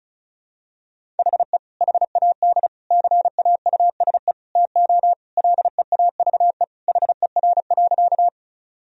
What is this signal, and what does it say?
Morse code, heard as a keyed tone, about 36 wpm, 690 Hz. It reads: HE HAD CAUSE TO LEAVE HER.
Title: Sentences from Top 1000 Words - Review Part 1 - 35wpm